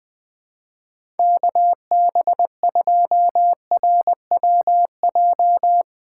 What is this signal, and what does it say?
Morse code, heard as a keyed tone, about 20 wpm, 700 Hz. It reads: KB2RWJ
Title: USA Call Signs - 6 Characters - 2x Character Spacing - 20wpm